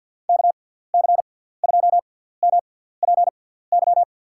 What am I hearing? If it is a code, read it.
Morse code, heard as a keyed tone, about 52 wpm, spaced as for 17 wpm, 700 Hz. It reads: KC2MPY
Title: USA Call Signs - 6 Characters - 6x Character Spacing - 50wpm